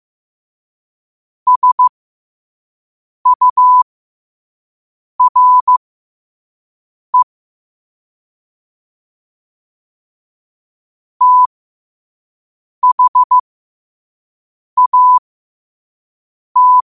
Morse code, heard as a keyed tone, about 15 wpm, spaced as for 5 wpm, 1000 Hz. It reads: SURE THAT
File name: qotdfiles / 5wpm